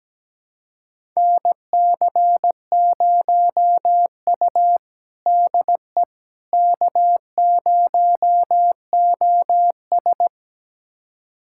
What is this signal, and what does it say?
Morse code, heard as a keyed tone, about 17 wpm, 700 Hz. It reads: NC0U DE K0OS
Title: USA Callsign DE Callsign - 4 Characters - 17wpm